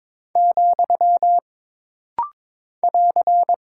Morse code, heard as a keyed tone, about 22 wpm, 700 Hz. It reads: , <AR>
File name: practice / 22wpm / Single Punctuation-Prosign - Rapid-Fire - 22wpm